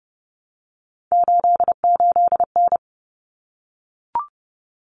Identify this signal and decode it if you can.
Morse code, heard as a keyed tone, about 30 wpm, 700 Hz. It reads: 88N
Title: Common Cut Numbers - 3-Digit 30wpm